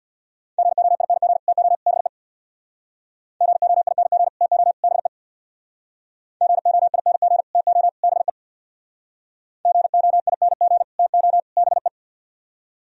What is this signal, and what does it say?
Morse code, heard as a keyed tone, about 64 wpm, 700 Hz. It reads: GOING TO BE GOING TO BE GOING TO BE GOING TO BE